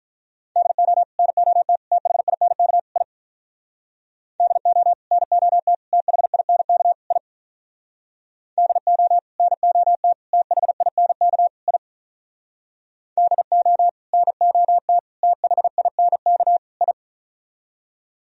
Morse code, heard as a keyed tone, about 47 wpm, 700 Hz. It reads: DO NOT THINK I DO NOT THINK I DO NOT THINK I DO NOT THINK I